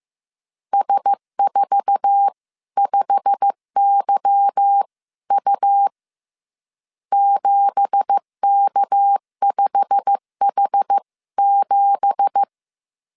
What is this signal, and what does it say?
Morse code, heard as a keyed tone, about 15 wpm, spaced as for 13 wpm, 800 Hz. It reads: S45YU 7K5H7
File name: CW / DET / 1 9 1500C S C 14wpm 3245